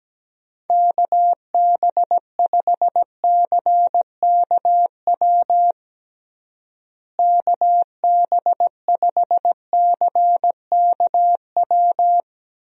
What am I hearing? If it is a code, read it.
Morse code, heard as a keyed tone, about 17 wpm, 700 Hz. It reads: KB5CKW KB5CKW
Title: USA Call Signs - 6 Characters - Repeated 2x - 17wpm